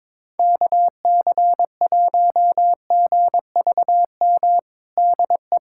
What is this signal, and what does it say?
Morse code, heard as a keyed tone, about 22 wpm, 700 Hz. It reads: KC1GVM DE
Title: USA Callsign DE Callsign 22wpm